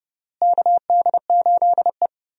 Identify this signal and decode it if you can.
Morse code, heard as a keyed tone, about 31 wpm, 700 Hz. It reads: KD8E
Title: USA Call Signs - 4 Characters - Repeated 2x - 30wpm